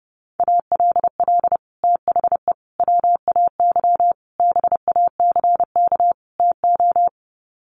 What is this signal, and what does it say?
Morse code, heard as a keyed tone, about 30 wpm, 700 Hz. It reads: ALL THE WAY BACK TO
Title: Sets of 5 Words Encore 30wpm